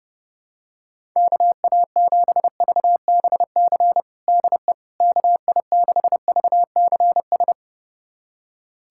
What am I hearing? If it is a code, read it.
Morse code, heard as a keyed tone, about 32 wpm, 700 Hz. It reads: KA7VBC DE KI6VCS
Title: USA Callsign DE Callsign - 6 Characters - 30wpm